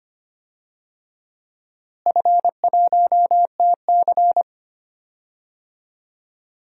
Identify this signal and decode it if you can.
Morse code, heard as a keyed tone, about 25 wpm, 700 Hz. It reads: F1TC
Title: DXCC-100 Call Signs - 4 Characters - Repeated 2x - 25wpm